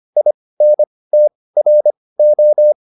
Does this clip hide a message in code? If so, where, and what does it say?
Yes: Morse code from 0.2 to 2.7 s, heard as a keyed tone, about 25 wpm, spaced as for 18 wpm, 600 Hz. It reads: INTRO